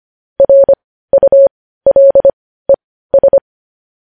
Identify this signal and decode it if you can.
Morse code, heard as a keyed tone, about 25 wpm, spaced as for 15 wpm, 550 Hz. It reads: RULES
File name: podcast / News headlines-1x-25x15